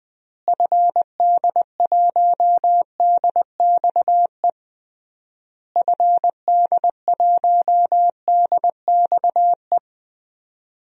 Morse code, heard as a keyed tone, about 20 wpm, 700 Hz. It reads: FD1DXE FD1DXE
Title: DXCC-100 Call Signs - 6 Characters - Repeated 3x - 20wpm